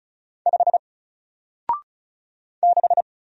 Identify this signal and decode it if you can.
Morse code, heard as a keyed tone, about 34 wpm, 700 Hz. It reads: 5 6
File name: Single-Letter-Number-35wpm